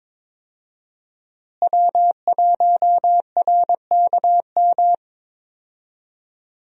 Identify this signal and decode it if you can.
Morse code, heard as a keyed tone, about 22 wpm, 700 Hz. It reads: W1RKM
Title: USA Call Signs - 5 Characters - Repeated 2x - 22wpm